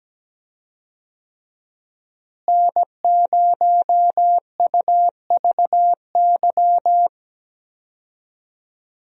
Morse code, heard as a keyed tone, about 17 wpm, 700 Hz. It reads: N0UVY
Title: USA Call Signs - 5 Characters - Repeated 2x - 17wpm